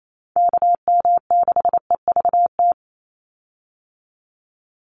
Morse code, heard as a keyed tone, about 28 wpm, 700 Hz. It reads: KM6EVT
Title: USA Call Signs 28wpm